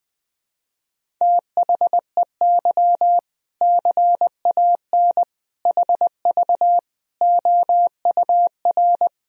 Morse code, heard as a keyed tone, about 20 wpm, 700 Hz. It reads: THEY CAN HV OUR
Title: Sentences from Top 400 Words and Abbreviations 20wpm